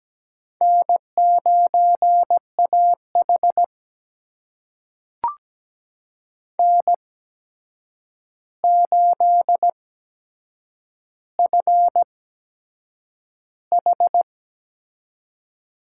Morse code, heard as a keyed tone, about 17 wpm, spaced as for 7 wpm, 700 Hz. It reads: N9AH N8FH
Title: USA Call Signs - 8x Character Spacing - 17wpm